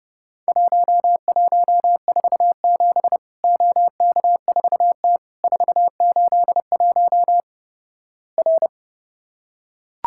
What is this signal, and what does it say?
Morse code, heard as a keyed tone, about 29 wpm, 700 Hz. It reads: 1147 OK4T 481 R
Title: Worked All Europe DX Contest - QTC Exchange - 30wpm